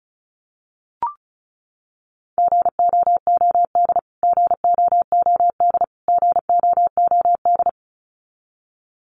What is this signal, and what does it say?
Morse code, heard as a keyed tone, about 34 wpm, 700 Hz. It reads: GOOD GOOD GOOD
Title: Top 500 Words Repeated 35wpm